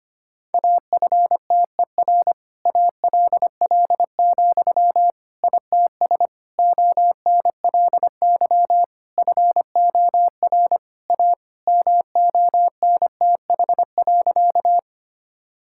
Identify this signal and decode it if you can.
Morse code, heard as a keyed tone, about 25 wpm, 700 Hz. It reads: AFTER ALL, ITS ONLY FOR A MONTH.
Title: Sentences from Top 1000 Words - Review Part 5 - 25wpm